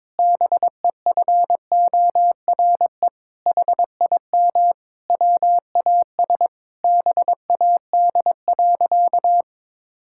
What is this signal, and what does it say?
Morse code, heard as a keyed tone, about 22 wpm, 700 Hz. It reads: BEFORE HIM WAS BAD.
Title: Sentences from Top 1600 Words 22wpm